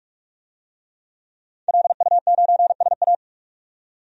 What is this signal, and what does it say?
Morse code, heard as a keyed tone, about 44 wpm, 700 Hz. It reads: LU9SA
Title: DXCC-100 Call Signs - 5 Characters - 45wpm